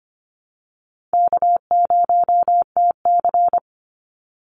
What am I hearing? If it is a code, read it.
Morse code, heard as a keyed tone, about 24 wpm, 700 Hz. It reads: K0TC
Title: Call Signs Repeated 3x - 25wpm